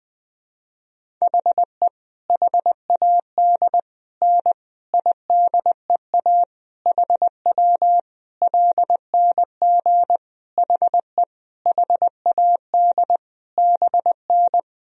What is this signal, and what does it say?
Morse code, heard as a keyed tone, about 21 wpm, 700 Hz. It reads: HE HAD N IDEA HW LNG HE HAD BN
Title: Sentences from Top 300 Words and Abbreviations 20wpm